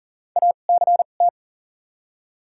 Morse code, heard as a keyed tone, about 40 wpm, spaced as for 29 wpm, 700 Hz. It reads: ACT
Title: 2x Character Spacing - Top 200 Words - 40wpm